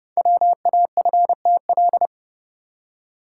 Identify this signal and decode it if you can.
Morse code, heard as a keyed tone, about 29 wpm, 690 Hz. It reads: WAFTL
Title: Code Groups - 5 Letters - 30wpm